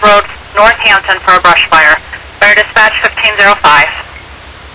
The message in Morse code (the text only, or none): none